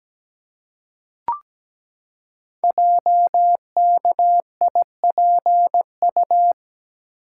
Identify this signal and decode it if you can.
Morse code, heard as a keyed tone, about 17 wpm, 700 Hz. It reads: JKIPU
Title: Code Groups - 5 Letters - 17wpm